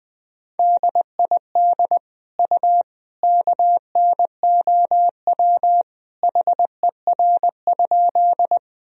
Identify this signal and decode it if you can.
Morse code, heard as a keyed tone, about 20 wpm, 700 Hz. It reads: DID U KNOW HER?
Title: Sentences from Top 100 Words and Abbreviations 20wpm